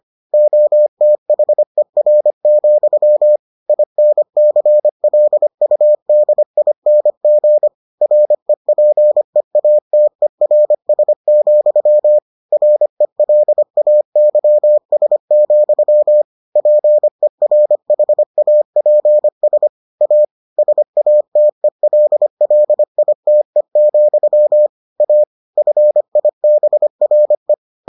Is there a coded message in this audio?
Yes: Morse code from 0.3 to 27.5 s, heard as a keyed tone, about 25 wpm, 600 Hz. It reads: OTHER, INCLUDING REPEATERS, RELAYS, PERHAPS A SATELLITE, A FIBRE